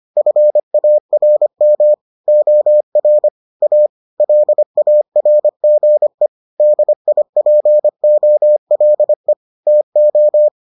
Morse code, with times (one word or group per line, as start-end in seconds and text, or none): 0.2-1.9 FARM
2.3-3.3 OR
3.6-3.9 A
4.2-6.3 LARGE
6.6-9.3 DIPOLE
9.7-10.5 TO